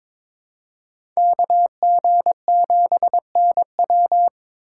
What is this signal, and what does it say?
Morse code, heard as a keyed tone, about 22 wpm, 700 Hz. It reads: KG7NW
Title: USA Call Signs - 5 Characters - 3x Character Spacing - 22wpm